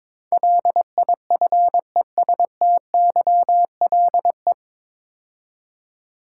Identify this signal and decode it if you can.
Morse code, heard as a keyed tone, about 22 wpm, 700 Hz. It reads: LIFESTYLE
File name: 9 Letter Words - First 3 Letters Given - 22wpm